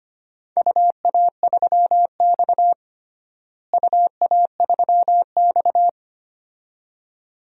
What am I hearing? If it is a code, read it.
Morse code, heard as a keyed tone, about 25 wpm, 700 Hz. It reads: UA3X UA3X
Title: DXCC-100 Call Signs - 4 Characters - Repeated 2x - 25wpm